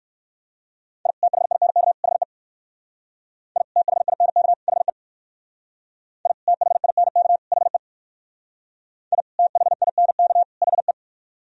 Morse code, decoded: I THINK HE I THINK HE I THINK HE I THINK HE

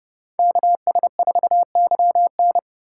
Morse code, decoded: KS4YN

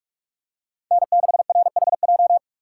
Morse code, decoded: N6RHJ

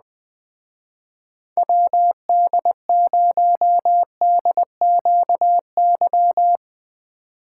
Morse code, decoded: WD0DQY

